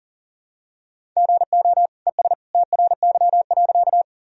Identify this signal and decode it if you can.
Morse code, heard as a keyed tone, about 38 wpm, 700 Hz. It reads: GO ES TRY.
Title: Sentences from Top 200 Words and Abbreviations 40wpm